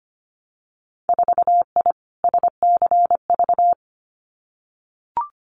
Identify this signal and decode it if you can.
Morse code, heard as a keyed tone, about 25 wpm, 700 Hz. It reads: 4I SCV
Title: Winter Field Day Exchanges Encore 25wpm